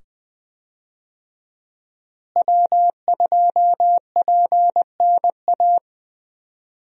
Morse code, heard as a keyed tone, about 20 wpm, 700 Hz. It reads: W2PNA
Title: USA Call Signs - 5 Characters - Repeated 2x - 20wpm